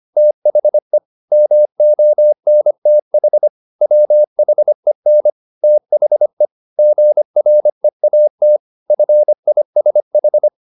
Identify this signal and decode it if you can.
Morse code, heard as a keyed tone, about 25 wpm, 600 Hz. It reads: THE MONTH WHEN THE GREAT FISH
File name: the old man and the sea part 1 25wpm